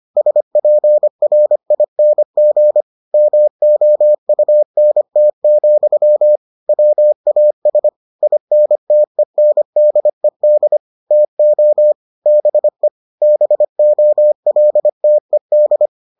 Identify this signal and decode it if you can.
Morse code, heard as a keyed tone, about 25 wpm, 600 Hz. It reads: SPRING MOUNT, WAS INTENDED TO BE BOLTED